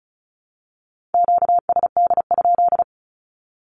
Morse code, heard as a keyed tone, about 36 wpm, 700 Hz. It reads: QSD?
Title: Q-Codes with Spoken Meaning - 35wpm